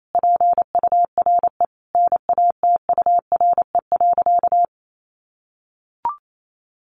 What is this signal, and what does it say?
Morse code, heard as a keyed tone, about 28 wpm, 700 Hz. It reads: PURE NATURE.
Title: Sentences from Top 1500 Words 28wpm